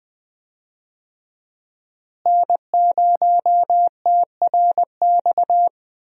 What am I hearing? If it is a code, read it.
Morse code, heard as a keyed tone, about 20 wpm, 700 Hz. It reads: N0TRX
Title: USA Call Signs - 5 Characters - Repeated 2x - 20wpm